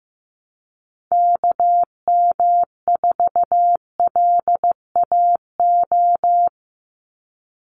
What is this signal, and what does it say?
Morse code, heard as a keyed tone, about 15 wpm, 700 Hz. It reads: KM4LAO
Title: Call Signs Repeated 3x - 15wpm